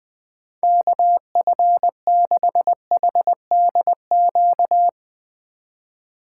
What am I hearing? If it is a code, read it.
Morse code, heard as a keyed tone, about 21 wpm, 700 Hz. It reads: KF6HDQ
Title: Speed Racing - USA 2x3 Call Signs 20wpm